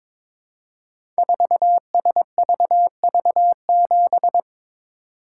Morse code, heard as a keyed tone, about 22 wpm, 700 Hz. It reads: 4SVV7
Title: Code Groups - 5 Letter:Numbers - 22wpm